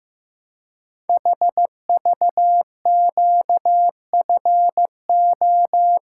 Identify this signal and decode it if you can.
Morse code, heard as a keyed tone, about 15 wpm, 700 Hz. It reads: HVQFO
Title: Code Groups - 5 Letters - 15wpm